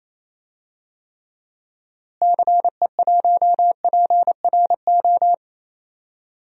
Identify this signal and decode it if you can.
Morse code, heard as a keyed tone, about 28 wpm, 700 Hz. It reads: CE1PRO
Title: DXCC-100 Call Signs - 6 Characters - Repeated 2x - 28wpm